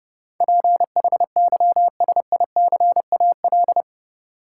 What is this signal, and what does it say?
Morse code, heard as a keyed tone, about 30 wpm, 700 Hz. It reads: PHYSICAL
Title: Sentences from Top 1000 Words - Review Part 4 - New Word Drill - 30wpm